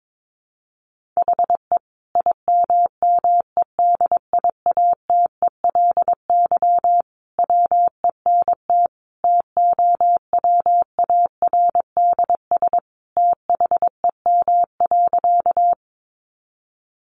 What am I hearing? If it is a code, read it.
Morse code, heard as a keyed tone, about 22 wpm, 700 Hz. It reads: HE IMMEDIATELY WENT TOWARDS THEM.
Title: Sentences from Top 1100 Words 22wpm